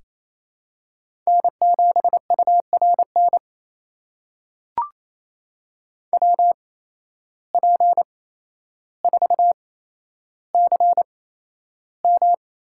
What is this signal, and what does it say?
Morse code, heard as a keyed tone, about 28 wpm, spaced as for 12 wpm, 700 Hz. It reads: N7URN WP4CM
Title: USA Call Signs - 5 Characters - 8x Character Spacing - 28wpm